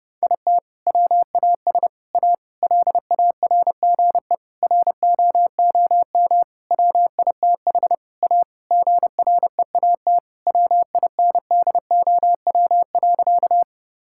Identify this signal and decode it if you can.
Morse code, heard as a keyed tone, about 30 wpm, 700 Hz. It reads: IT WAS A LARGE ROOM WITH A GREAT WINDOW.
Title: Sentences from Top 1600 Words 30wpm